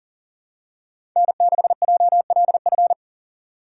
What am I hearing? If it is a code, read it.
Morse code, heard as a keyed tone, about 40 wpm, 700 Hz. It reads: N6JLF
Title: USA Call Signs - 5 Characters - 3x Character Spacing - 40wpm